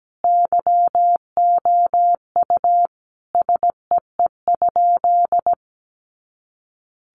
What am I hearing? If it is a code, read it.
Morse code, heard as a keyed tone, about 17 wpm, 700 Hz. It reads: YOU SEE?
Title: Sentences from Top 100 Words 17wpm